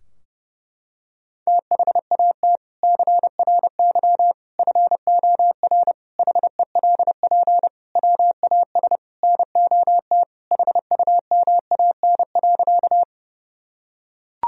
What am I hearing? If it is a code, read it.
Morse code, heard as a keyed tone, about 30 wpm, 700 Hz. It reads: THAT CRY FOR HELP WAS NOT HUMAN.
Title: Sentences from Top 1000 Words - Review Part 3 - 30wpm